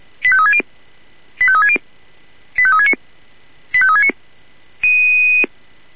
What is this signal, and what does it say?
Five-tone selective call (ZVEI), heard as quick runs of five tones, tones of about 70 ms. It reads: 85360 85360 85369 85369